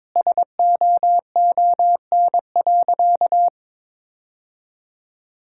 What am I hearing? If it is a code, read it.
Morse code, heard as a keyed tone, about 22 wpm, 700 Hz. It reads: SOON.
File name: practice / 22wpm / Sentences with 3-Word Phrases - 22wpm